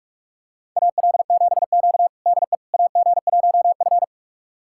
Morse code, heard as a keyed tone, about 44 wpm, 690 Hz. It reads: AL7Q DE AG1F
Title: USA Callsign DE Callsign - 4 Characters - 45wpm